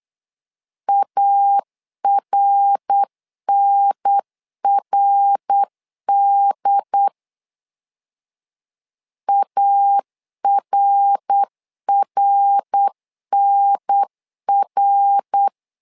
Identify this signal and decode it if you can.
Morse code, heard as a keyed tone, about 8 wpm, 800 Hz. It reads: ARNRD ARRNR
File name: LEZIONI / 3 MWRO / 3 MWR 8wpm 2444